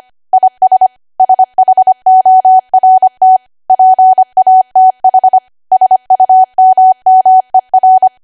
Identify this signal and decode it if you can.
Morse code, heard as a keyed tone, about 25 wpm, 750 Hz. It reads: IS SHORT PATH SUMMER